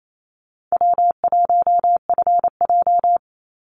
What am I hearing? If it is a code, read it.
Morse code, heard as a keyed tone, about 28 wpm, 700 Hz. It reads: W1FJ
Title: Easy Call Signs - 28wpm